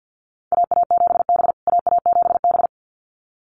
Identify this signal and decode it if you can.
Morse code, heard as a keyed tone, about 50 wpm, 700 Hz. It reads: AA7B AA7B
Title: USA Call Signs Repeated 2x 50wpm